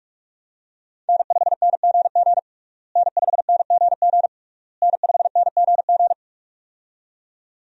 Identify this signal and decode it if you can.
Morse code, heard as a keyed tone, about 46 wpm, 700 Hz. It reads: N5NGG N5NGG N5NGG
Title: USA Call Signs - 5 Characters - Repeated 3x - 45wpm